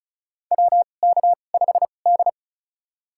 Morse code, heard as a keyed tone, about 36 wpm, spaced as for 25 wpm, 700 Hz. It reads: WK5D
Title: USA Call Signs - 4 Characters - 2x Character Spacing - 35wpm